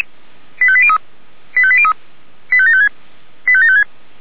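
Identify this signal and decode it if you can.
Five-tone selective call (ZVEI), heard as quick runs of five tones, tones of about 70 ms. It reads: 85803 85803 85756 85756
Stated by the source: ZVEI five-tone selective call received on an SDR by the Feuerwehr Orlamünde fire service.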